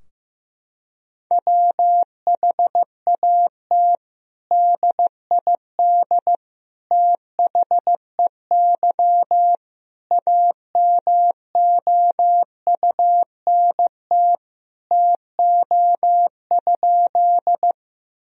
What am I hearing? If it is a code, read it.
Morse code, heard as a keyed tone, about 16 wpm, 700 Hz. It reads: WHAT DID THEY AMOUNT TO?
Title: Sentences from Top 1000 Words - Review Part 4 - 15wpm